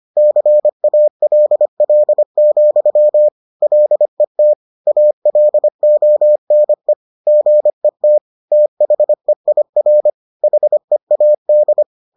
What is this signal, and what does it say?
Morse code, heard as a keyed tone, about 25 wpm, 600 Hz. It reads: CALL, LET ALONE GET THEIR HEAD